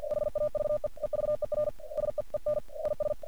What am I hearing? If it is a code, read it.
Morse code, heard as a keyed tone, about 49 wpm, 600 Hz. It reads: HAVE EVER SEEN IS
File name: archived / CRSnet